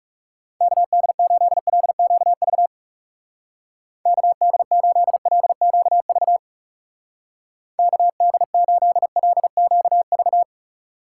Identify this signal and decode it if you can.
Morse code, heard as a keyed tone, about 35 wpm, 700 Hz. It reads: KD8LQV KD8LQV KD8LQV
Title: Speed Racing - USA 2x3 Call Signs 30wpm